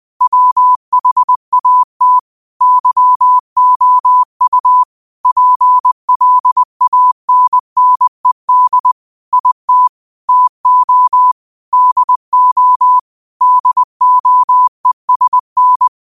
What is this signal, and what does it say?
Morse code, heard as a keyed tone, about 20 wpm, 1000 Hz. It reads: WHAT YOU PLANNED IT TO DO DOESN
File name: qotdfiles / 20wpm